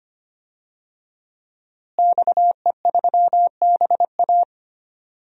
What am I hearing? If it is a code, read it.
Morse code, heard as a keyed tone, about 25 wpm, 700 Hz. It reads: XE3BA
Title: DXCC-100 Call Signs - 5 Characters - Repeated 3x - 25wpm